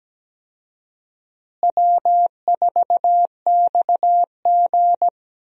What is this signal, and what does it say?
Morse code, heard as a keyed tone, about 17 wpm, 700 Hz. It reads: W4XG
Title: USA Call Signs - 4 Characters - Repeated 2x - 17wpm